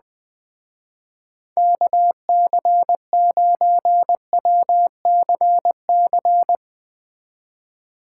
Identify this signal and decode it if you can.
Morse code, heard as a keyed tone, about 20 wpm, 700 Hz. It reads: KC9WCC
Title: Speed Racing - USA 2x3 Call Signs 20wpm